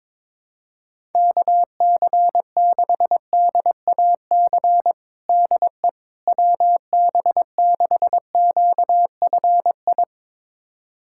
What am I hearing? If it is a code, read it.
Morse code, heard as a keyed tone, about 22 wpm, 700 Hz. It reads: KC6DAC DE WB6QFI
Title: USA Callsign DE Callsign - 6 Characters - 22wpm